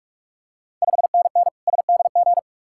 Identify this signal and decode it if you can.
Morse code, heard as a keyed tone, about 44 wpm, 690 Hz. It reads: 5NN SDG
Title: Ontario QSO Party Exchange 45wpm